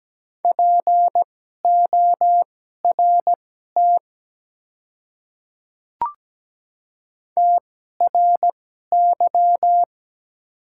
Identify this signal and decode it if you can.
Morse code, heard as a keyed tone, about 17 wpm, spaced as for 12 wpm, 690 Hz. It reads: PORT TRY
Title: 2x Character Spacing - Top 300 Words - 17wpm